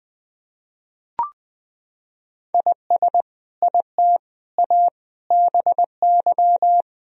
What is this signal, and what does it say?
Morse code, heard as a keyed tone, about 20 wpm, 700 Hz. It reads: IS IT A BY
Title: Sentences from Top 1000 Words - Review Part 2 - 20wpm